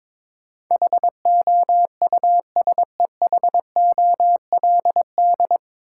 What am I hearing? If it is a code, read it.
Morse code, heard as a keyed tone, about 22 wpm, 700 Hz. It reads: HOUSEHOLD
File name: 9 Letter Words - First 3 Letters Given - 22wpm